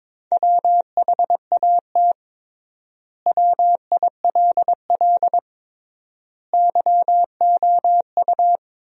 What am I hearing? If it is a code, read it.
Morse code, heard as a keyed tone, about 22 wpm, 700 Hz. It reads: WHAT WILL YOU
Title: 3x Word-Spacing - Sentences from Top 100 Words - 22wpm